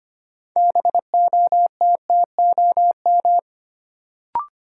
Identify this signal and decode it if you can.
Morse code, heard as a keyed tone, about 25 wpm, 700 Hz. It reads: BOTTOM
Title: Sets of 3 Words - New Word Drill - 25wpm